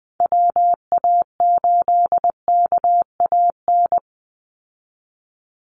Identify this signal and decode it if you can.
Morse code, heard as a keyed tone, about 20 wpm, 700 Hz. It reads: WA8KAN
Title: Call Signs Repeated 3x - 20wpm